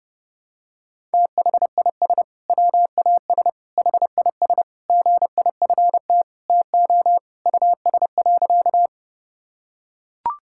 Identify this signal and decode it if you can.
Morse code, heard as a keyed tone, about 30 wpm, 700 Hz. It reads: THIS WAS HIS GIFT TO US.
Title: Sentences from Top 1000 Words - Review Part 6 - 30wpm